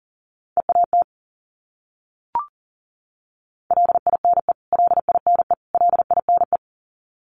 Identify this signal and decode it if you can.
Morse code, heard as a keyed tone, about 40 wpm, 700 Hz. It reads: EAT LINE LINE LINE
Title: Top 500 Words Repeated 40wpm